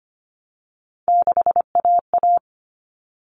Morse code, heard as a keyed tone, about 25 wpm, 700 Hz. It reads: 6AA
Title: Common Cut Numbers - 3-Digit 25wpm